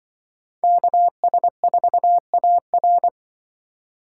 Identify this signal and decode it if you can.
Morse code, heard as a keyed tone, about 24 wpm, 700 Hz. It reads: KS4AR
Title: Speed Racing - USA 2x2 Call Signs 20wpm